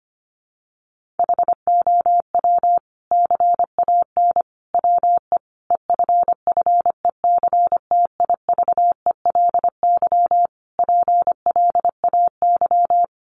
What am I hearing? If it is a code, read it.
Morse code, heard as a keyed tone, about 25 wpm, 700 Hz. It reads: HOW CAN WE EFFECTIVELY PLAY
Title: Sentences from Top 1400 Words 25wpm